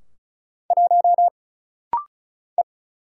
Morse code, heard as a keyed tone, about 36 wpm, 700 Hz. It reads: 1 E
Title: Single Letter-Number - Rapid-Fire 35wpm